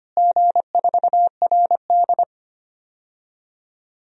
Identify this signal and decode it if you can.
Morse code, heard as a keyed tone, about 25 wpm, 700 Hz. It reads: G4RD
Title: DXCC-100 Call Signs - 4 Characters - Repeated 3x - 25wpm